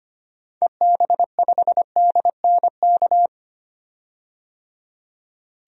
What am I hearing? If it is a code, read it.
Morse code, heard as a keyed tone, about 25 wpm, 700 Hz. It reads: EB5DNK
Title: DXCC-100 Call Signs - 6 Characters - 25wpm